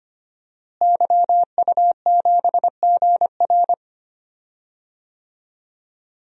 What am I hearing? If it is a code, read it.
Morse code, heard as a keyed tone, about 25 wpm, 700 Hz. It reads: YU7GR
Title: DXCC-100 Call Signs - 5 Characters - Repeated 2x - 25wpm